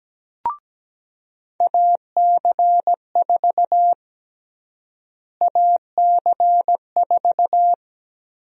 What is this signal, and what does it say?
Morse code, heard as a keyed tone, about 17 wpm, 700 Hz. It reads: AC4 AC4
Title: USA Call Sign Prefixes - Repeated 2x - 17wpm